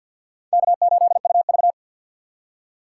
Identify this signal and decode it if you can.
Morse code, heard as a keyed tone, about 53 wpm, 700 Hz. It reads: K8UV
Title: USA Call Signs - 4 Characters - 6x Character Spacing - 50wpm